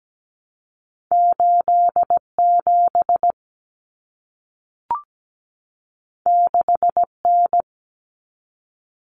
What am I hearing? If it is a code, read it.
Morse code, heard as a keyed tone, about 17 wpm, 700 Hz. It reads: 87 6N
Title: Common Cut Numbers - 2-Digit 17wpm